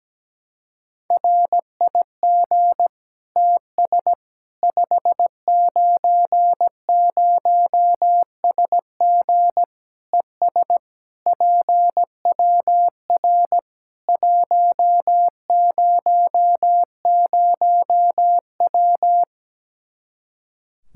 Morse code, heard as a keyed tone, about 17 wpm, 700 Hz. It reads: RIG TS 590SG ES PWR 100W
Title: Practice QSO - Exchange 2 - Info - 17wpm